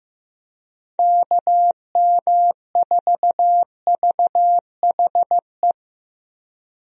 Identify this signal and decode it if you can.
Morse code, heard as a keyed tone, about 15 wpm, 700 Hz. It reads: KM4VHE
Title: USA Call Signs - 6 Characters - Repeated 2x - 15wpm